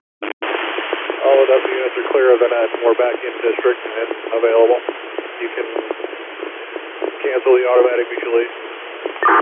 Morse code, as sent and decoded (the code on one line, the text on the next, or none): none